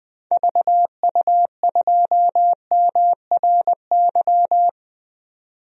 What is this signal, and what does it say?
Morse code, heard as a keyed tone, about 21 wpm, 700 Hz. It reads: VU2MRY
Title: DXCC-100 Call Signs - 6 Characters - 20wpm